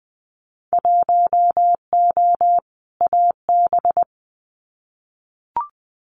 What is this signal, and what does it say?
Morse code, heard as a keyed tone, about 20 wpm, 700 Hz. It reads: 1O AB
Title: Winter Field Day Exchanges 20wpm